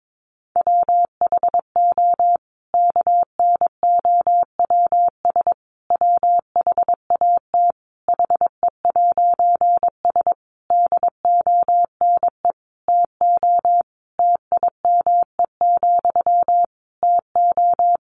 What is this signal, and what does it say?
Morse code, heard as a keyed tone, about 22 wpm, 700 Hz. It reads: WHO KNOWS WHAT HE'S DONE TO TIME, TO
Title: A Sound Of Thunder 22wpm